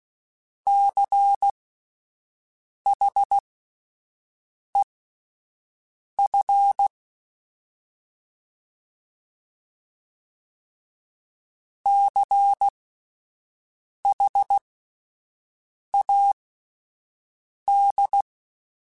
Morse code, heard as a keyed tone, about 16 wpm, spaced as for 5 wpm, 800 Hz. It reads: CHEF CHAD
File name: Spaced Alphas / Spaced Alphas mp3s / Cs sp @10